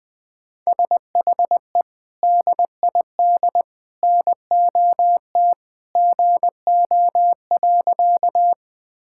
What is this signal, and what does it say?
Morse code, heard as a keyed tone, about 20 wpm, 700 Hz. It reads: SHE DID NOT GO.
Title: Sentences with 3-Word Phrases - 20wpm